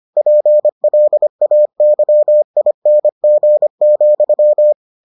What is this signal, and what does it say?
Morse code, heard as a keyed tone, about 25 wpm, 600 Hz. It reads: PLAYING,